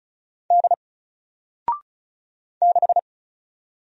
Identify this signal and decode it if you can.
Morse code, heard as a keyed tone, about 34 wpm, 700 Hz. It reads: D 6
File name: Single-Letter-Number-35wpm